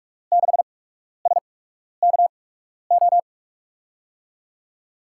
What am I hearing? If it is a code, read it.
Morse code, heard as a keyed tone, about 44 wpm, spaced as for 12 wpm, 700 Hz. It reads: 6SKO
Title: USA Call Signs - 6 Characters - 8x Character Spacing - 45wpm